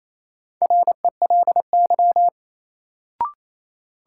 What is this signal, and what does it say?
Morse code, heard as a keyed tone, about 28 wpm, 700 Hz. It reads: RELY
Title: Sentences from Top 1200 Words - New Word Drill Alt - 28wpm